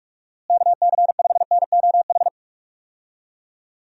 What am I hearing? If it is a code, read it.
Morse code, heard as a keyed tone, about 46 wpm, 700 Hz. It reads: KC5NOH
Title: USA Call Signs - 6 Characters - Repeated 3x - 45wpm